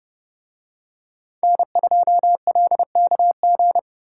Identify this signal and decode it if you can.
Morse code, heard as a keyed tone, about 28 wpm, 700 Hz. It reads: N2LKG